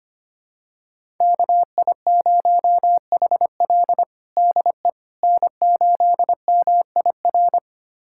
Morse code, heard as a keyed tone, about 25 wpm, 700 Hz. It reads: KI0HL DE N8MIR
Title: USA Callsign DE Callsign - 5 Characters - 25wpm